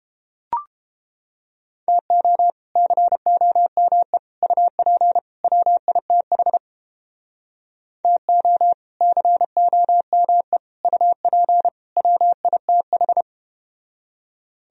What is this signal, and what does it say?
Morse code, heard as a keyed tone, about 33 wpm, 700 Hz. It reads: TO COME UP WITH TO COME UP WITH